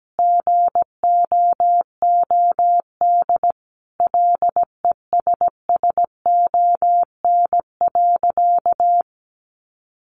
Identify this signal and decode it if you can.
Morse code, heard as a keyed tone, about 17 wpm, 700 Hz. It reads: GOOD LESSON.
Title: Sentences from Top 1200 Words 17wpm